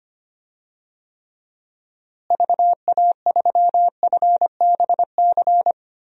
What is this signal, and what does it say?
Morse code, heard as a keyed tone, about 25 wpm, 700 Hz. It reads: VA3FBC
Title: DXCC-100 Call Signs - 6 Characters - Repeated 2x - 25wpm